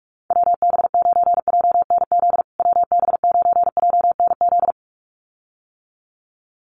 Morse code, heard as a keyed tone, about 44 wpm, 700 Hz. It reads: WB9JNZ WB9JNZ
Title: USA Call Signs Repeated 2x 45wpm